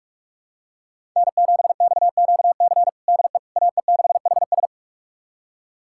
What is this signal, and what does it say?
Morse code, heard as a keyed tone, about 47 wpm, 700 Hz. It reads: N7XQC DE AE6HS
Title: USA Callsign DE Callsign - 5 Characters - 45wpm